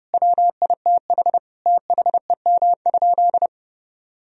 Morse code, heard as a keyed tone, about 30 wpm, 700 Hz. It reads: WITH THEM?
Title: Sentences from Top 1000 Words - Review Part 2 - 30wpm